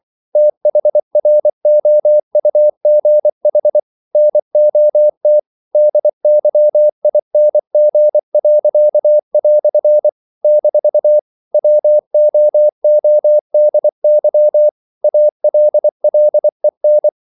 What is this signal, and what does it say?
Morse code, heard as a keyed tone, about 24 wpm, 600 Hz. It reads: THROUGH NOT DYING." - WOODY ALLEN
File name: CW / Practice / EnglishQuotations0424